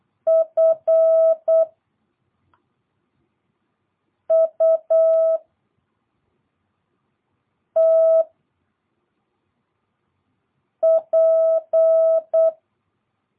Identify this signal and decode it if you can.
Morse code, heard as a keyed tone, about 8 wpm, 650 Hz. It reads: F U T P